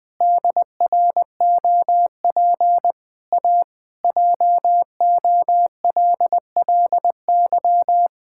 Morse code, heard as a keyed tone, about 21 wpm, 700 Hz. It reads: DROP A JOLLY
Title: Pangram Challenge v2 - 20wpm